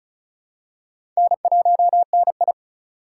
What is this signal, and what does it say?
Morse code, heard as a keyed tone, about 36 wpm, 700 Hz. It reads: N1NI
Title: USA Call Signs - 4 Characters - 3x Character Spacing - 35wpm